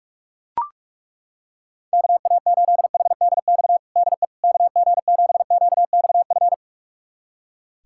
Morse code, heard as a keyed tone, about 42 wpm, 690 Hz. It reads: KA8HDX DE KG7QXF
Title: USA Callsign DE Callsign - 6 Characters - 45wpm